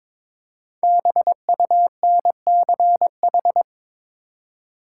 Morse code, heard as a keyed tone, about 22 wpm, 700 Hz. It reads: BUNCH
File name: 5 Letter Words - First 3 Letters Given - 22wpm